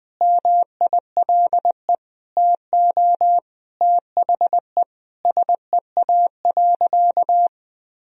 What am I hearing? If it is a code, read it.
Morse code, heard as a keyed tone, about 20 wpm, 700 Hz. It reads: MILE TO THE SEA.
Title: Sentences from Top 1000 Words - Review Part 1 - 20wpm